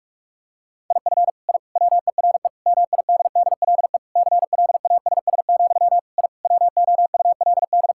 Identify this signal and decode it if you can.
Morse code, heard as a keyed tone, about 41 wpm, 700 Hz. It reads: IF I WERE MIDDLE CLASS, I WOULD
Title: Sentences from Top 1000 Words - Review Part 6 - 45wpm